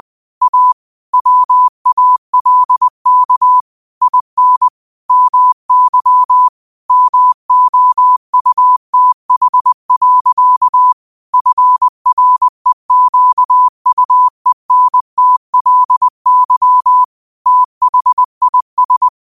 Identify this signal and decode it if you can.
Morse code, heard as a keyed tone, about 20 wpm, 1000 Hz. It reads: A WALK IN MY MOUTH. FREQUENTLY THIS